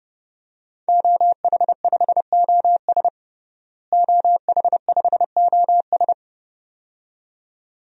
Morse code, heard as a keyed tone, about 30 wpm, 700 Hz. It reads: OH5OS OH5OS
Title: DXCC-100 Call Signs - 5 Characters - Repeated 2x - 30wpm